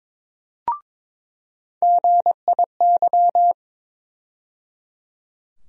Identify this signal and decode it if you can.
Morse code, heard as a keyed tone, about 22 wpm, 700 Hz. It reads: GIY
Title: USA Call Sign Postfixes - 3 Letters 22wpm